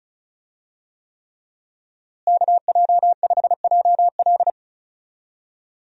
Morse code, heard as a keyed tone, about 36 wpm, 700 Hz. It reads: KJ5JL
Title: USA Call Signs - 5 Characters - Repeated 3x - 35wpm